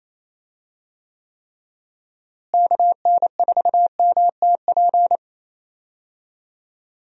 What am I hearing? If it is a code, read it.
Morse code, heard as a keyed tone, about 27 wpm, 700 Hz. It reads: KN4MTP
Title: USA Call Signs - 6 Characters - Repeated 3x - 28wpm